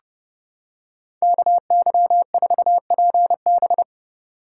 Morse code, heard as a keyed tone, about 30 wpm, 700 Hz. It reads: KY4PB